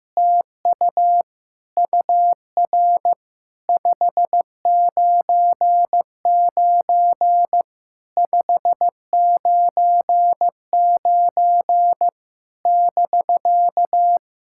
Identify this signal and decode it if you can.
Morse code, heard as a keyed tone, about 14 wpm, 700 Hz. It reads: TU UR 599 599 <BK>
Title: SOTA QSO Practice 15wpm